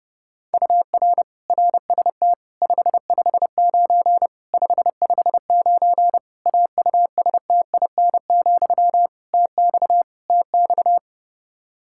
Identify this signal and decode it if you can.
Morse code, heard as a keyed tone, about 32 wpm, 700 Hz. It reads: UR RST 559 559 AUSTIN, TX TX
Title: K1USN SST QSO Practice 30wpm